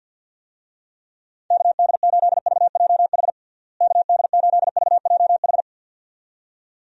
Morse code, heard as a keyed tone, about 51 wpm, 700 Hz. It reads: KD8VJH KD8VJH
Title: USA Call Signs - 6 Characters - Repeated 2x - 50wpm